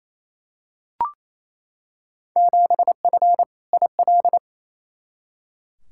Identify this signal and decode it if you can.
Morse code, heard as a keyed tone, about 28 wpm, 700 Hz. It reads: 7F IL
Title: Field Day Exchanges Encore - 28wpm